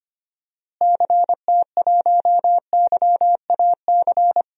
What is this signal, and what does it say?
Morse code, heard as a keyed tone, about 25 wpm, 700 Hz. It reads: CT1YAC
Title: DXCC-100 Call Signs - 6 Characters - Repeated 2x - 25wpm